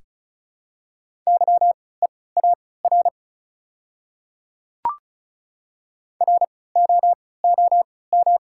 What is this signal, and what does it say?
Morse code, heard as a keyed tone, about 36 wpm, spaced as for 20 wpm, 700 Hz. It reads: YEAR ROOM
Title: 3x Character Spacing - Top 300 Words - 35wpm